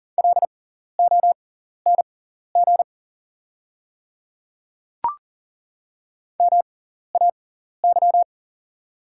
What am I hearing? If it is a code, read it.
Morse code, heard as a keyed tone, about 40 wpm, spaced as for 14 wpm, 690 Hz. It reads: LONG MAY
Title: 6x Character Spacing - Top 100 Words - 40wpm